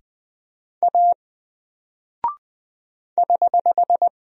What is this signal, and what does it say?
Morse code, heard as a keyed tone, about 20 wpm, 700 Hz. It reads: A <HH>
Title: Single Character (Letter:Number:Punctuation:Prosign) 20wpm